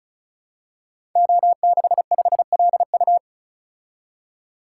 Morse code, heard as a keyed tone, about 36 wpm, 700 Hz. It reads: O65LU
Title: Code Groups - 5 Letter:Numbers - 35wpm